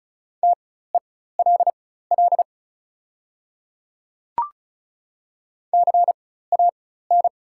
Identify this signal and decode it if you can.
Morse code, heard as a keyed tone, about 36 wpm, spaced as for 16 wpm, 700 Hz. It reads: TELL CAN
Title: 4x Character Spacing - Top 500 Words - 35wpm